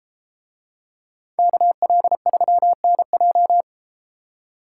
Morse code, heard as a keyed tone, about 32 wpm, 700 Hz. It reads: KL3NJ